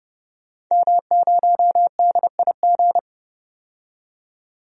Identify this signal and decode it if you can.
Morse code, heard as a keyed tone, about 30 wpm, 700 Hz. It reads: M0DIG
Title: DXCC-100 Call Signs - 5 Characters - 30wpm